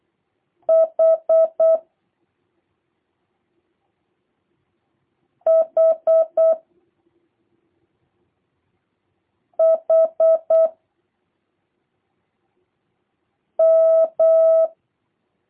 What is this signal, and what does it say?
Morse code, heard as a keyed tone, about 8 wpm, 650 Hz. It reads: H H H M
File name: jh8cbh / 2023cw / review1-2